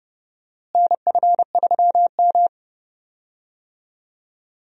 Morse code, heard as a keyed tone, about 29 wpm, 700 Hz. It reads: NF3M
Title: USA Call Signs - 4 Characters - Repeated 2x - 30wpm